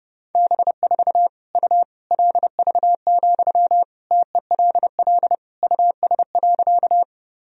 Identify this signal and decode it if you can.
Morse code, heard as a keyed tone, about 32 wpm, 700 Hz. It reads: B4 U LV, TELL US.
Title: Sentences from Top 400 Words and Abbreviations 30wpm